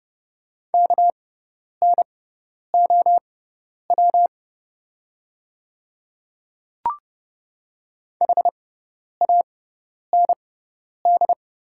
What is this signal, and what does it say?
Morse code, heard as a keyed tone, about 30 wpm, spaced as for 10 wpm, 700 Hz. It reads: KNOW HAND
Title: 6x Character Spacing - Top 300 Words - 30wpm